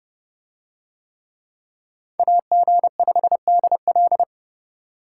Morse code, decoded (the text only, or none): AG5DL